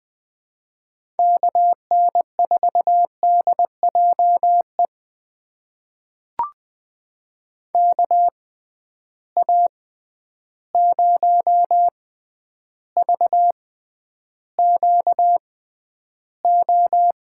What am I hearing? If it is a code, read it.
Morse code, heard as a keyed tone, about 21 wpm, spaced as for 10 wpm, 700 Hz. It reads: KN4DJE KA0VQO
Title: USA Call Signs - 6 Characters - 6x Character Spacing - 20wpm